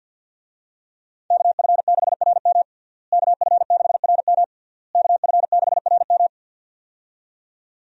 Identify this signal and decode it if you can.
Morse code, heard as a keyed tone, about 52 wpm, 700 Hz. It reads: KF6RM KF6RM KF6RM
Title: USA Call Signs - 5 Characters - Repeated 3x - 50wpm